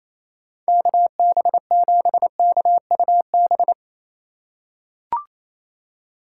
Morse code, heard as a keyed tone, about 28 wpm, 700 Hz. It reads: KB7KUB
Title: USA Call Signs - 6 Characters - 4x Character Spacing - 28wpm